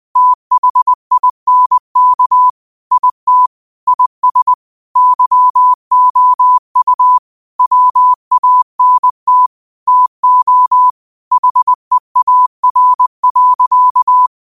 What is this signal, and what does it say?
Morse code, heard as a keyed tone, about 20 wpm, 1000 Hz. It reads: THINK IT IS YOU WANT TO HEAR.
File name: qotdfiles / 20wpm